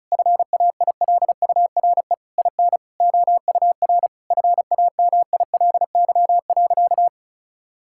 Morse code, heard as a keyed tone, about 36 wpm, 700 Hz. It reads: FAILURE IN OUR FAMILY.
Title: Sentences from Top 1000 Words - Review Part 6 - 35wpm